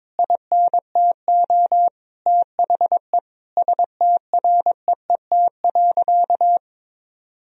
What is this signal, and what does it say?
Morse code, heard as a keyed tone, about 22 wpm, 700 Hz. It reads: INTO THE STREET.
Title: Sentences from Top 1000 Words - Review Part 4 - 22wpm